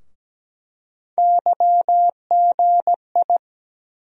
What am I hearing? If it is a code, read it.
Morse code, heard as a keyed tone, about 17 wpm, 700 Hz. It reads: YGI
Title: USA Call Sign Postfixes - 3 Letters 17wpm